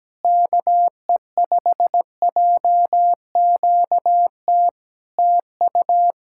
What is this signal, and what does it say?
Morse code, heard as a keyed tone, about 17 wpm, 700 Hz. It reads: KE5JQT TU
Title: Practice QSO - Exchange 4 - Ending - 17wpm